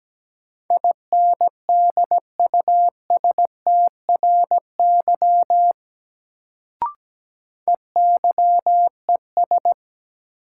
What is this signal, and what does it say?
Morse code, heard as a keyed tone, about 17 wpm, 700 Hz. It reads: INDUSTRY EYES
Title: Sentences from Top 1000 Words - Review Part 4 - New Word Drill - 17wpm